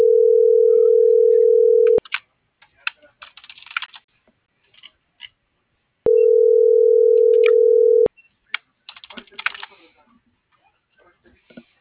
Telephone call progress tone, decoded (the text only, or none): ringback tone